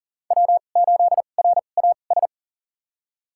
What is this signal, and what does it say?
Morse code, heard as a keyed tone, about 39 wpm, spaced as for 28 wpm, 700 Hz. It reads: W8RAS
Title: USA Call Signs - 5 Characters - 2x Character Spacing - 40wpm